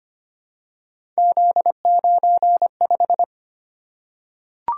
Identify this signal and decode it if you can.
Morse code, heard as a keyed tone, about 25 wpm, 700 Hz. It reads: Z95
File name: Three Letter-Number Combinations - 25wpm